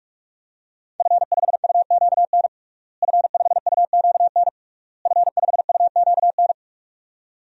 Morse code, heard as a keyed tone, about 41 wpm, 700 Hz. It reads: F5UQN F5UQN F5UQN
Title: DXCC-100 Call Signs - 5 Characters - Repeated 3x - 45wpm